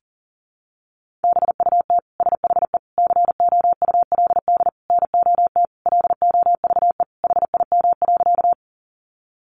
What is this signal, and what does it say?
Morse code, heard as a keyed tone, about 40 wpm, 700 Hz. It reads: BUT SHE COULD NOT LOVE HIM.
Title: Sentences from Top 600 Words 40wpm